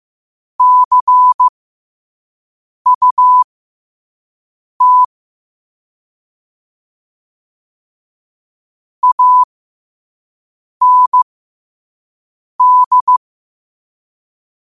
Morse code, heard as a keyed tone, about 15 wpm, spaced as for 5 wpm, 1000 Hz. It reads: CUT AND
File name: qotdfiles / 5wpm